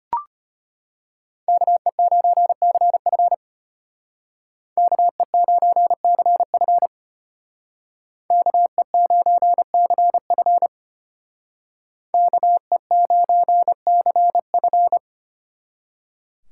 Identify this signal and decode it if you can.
Morse code, heard as a keyed tone, about 32 wpm, 700 Hz. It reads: KE9CF KE9CF KE9CF KE9CF